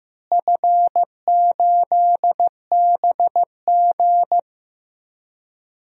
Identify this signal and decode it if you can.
Morse code, heard as a keyed tone, about 15 wpm, 700 Hz. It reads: F8BG
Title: DXCC-100 Call Signs - 4 Characters - Repeated 2x - 15wpm